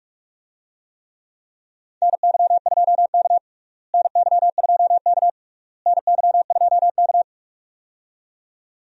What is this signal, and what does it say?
Morse code, heard as a keyed tone, about 47 wpm, 700 Hz. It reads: NY2K NY2K NY2K
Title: USA Call Signs - 4 Characters - Repeated 3x - 45wpm